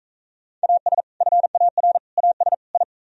Morse code, heard as a keyed tone, about 45 wpm, 700 Hz. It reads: AS FAR AS I